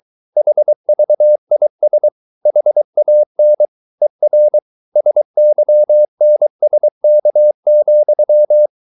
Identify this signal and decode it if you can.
Morse code, heard as a keyed tone, about 23 wpm, 600 Hz. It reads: HVIS HAN ER SYNSK,
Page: Ferie 23 wpm